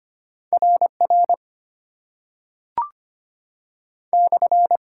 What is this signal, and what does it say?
Morse code, heard as a keyed tone, about 25 wpm, 690 Hz. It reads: RR /
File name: Top 90 QSO Elements - 8x Character Spacing - 25wpm